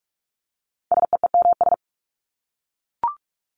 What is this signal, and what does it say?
Morse code, heard as a keyed tone, about 44 wpm, 700 Hz. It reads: SEEMS
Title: ES:S Suffix - Top 100 Words with Base Words 45wpm